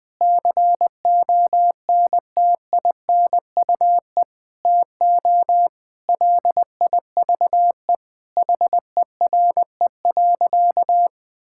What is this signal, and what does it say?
Morse code, heard as a keyed tone, about 19 wpm, 700 Hz. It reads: CONTINUE TO LIVE HERE.
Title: Sentences from Top 1000 Words - Review Part 6 - 20wpm